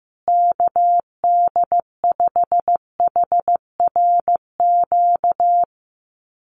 Morse code, heard as a keyed tone, about 15 wpm, 700 Hz. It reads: KD5HRQ
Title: USA Call Signs Repeated 3x 15wpm